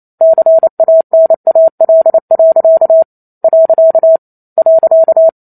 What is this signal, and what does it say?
Morse code, heard as a keyed tone, about 28 wpm, 650 Hz. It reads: CANAL. .
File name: Chapter 2 - 28 WPM